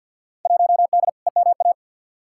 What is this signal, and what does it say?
Morse code, heard as a keyed tone, about 51 wpm, 700 Hz. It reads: 1D EMA
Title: Field Day Exchanges - 50wpm